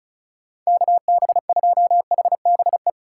Morse code, heard as a keyed tone, about 36 wpm, 700 Hz. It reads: KB2HBE
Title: USA Call Signs - 6 Characters - 3x Character Spacing - 35wpm